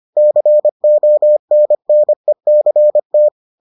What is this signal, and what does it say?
Morse code, heard as a keyed tone, about 25 wpm, 600 Hz. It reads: CONNECT